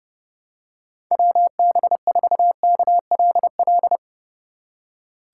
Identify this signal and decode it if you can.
Morse code, heard as a keyed tone, about 30 wpm, 700 Hz. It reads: WB4KLL
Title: USA Call Signs - 6 Characters - 2x Character Spacing - 30wpm